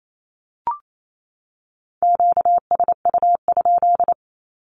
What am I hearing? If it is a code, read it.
Morse code, heard as a keyed tone, about 28 wpm, 700 Hz. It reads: QSU?
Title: Q-Codes 28wpm